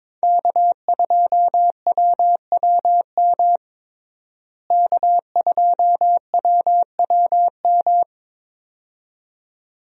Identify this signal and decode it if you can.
Morse code, heard as a keyed tone, about 21 wpm, 700 Hz. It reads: K2WWM K2WWM
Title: USA Call Signs - 5 Characters - Repeated 2x - 22wpm